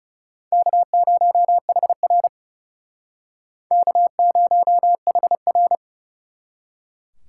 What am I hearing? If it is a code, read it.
Morse code, heard as a keyed tone, about 38 wpm, 700 Hz. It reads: K0HR K0HR